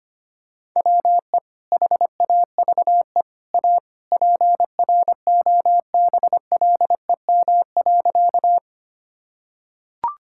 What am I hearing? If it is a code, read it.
Morse code, heard as a keyed tone, about 25 wpm, 700 Hz. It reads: WE HAVE A PROBLEM.
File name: practice / 25wpm / Sentences with 3-Word Phrases - 25wpm